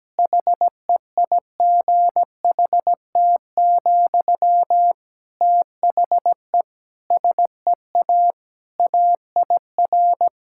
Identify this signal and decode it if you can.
Morse code, heard as a keyed tone, about 17 wpm, 700 Hz. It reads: HEIGHT, THE SEA AIR
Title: Sentences from Top 1600 Words 17wpm